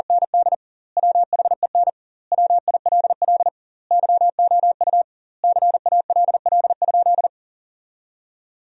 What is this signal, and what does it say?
Morse code, heard as a keyed tone, about 40 wpm, 700 Hz. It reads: ND WHEN WILL YOU CALL?
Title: 2x Word-Spacing - Sentences from Top 100 Words - 40wpm